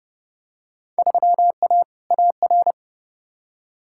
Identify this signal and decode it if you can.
Morse code, heard as a keyed tone, about 30 wpm, 700 Hz. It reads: 3A AR
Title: Field Day Exchanges - 30wpm